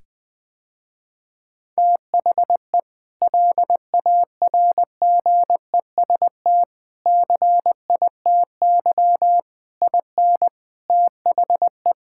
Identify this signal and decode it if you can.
Morse code, heard as a keyed tone, about 21 wpm, 700 Hz. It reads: THE LARGEST CITY IN THE
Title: Sentences with 5-Word Phrases - Spoken Beginning - 20wpm